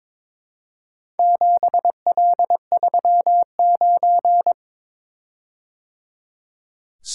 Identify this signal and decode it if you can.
Morse code, heard as a keyed tone, about 22 wpm, 700 Hz. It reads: 7L39
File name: practice / 22wpm / Four Letter-Number Combinations - 22wpm